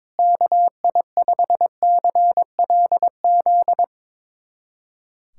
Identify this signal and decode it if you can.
Morse code, heard as a keyed tone, about 22 wpm, 700 Hz. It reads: KI5CLZ
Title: USA Callsign DE Callsign - 6 Characters - 22wpm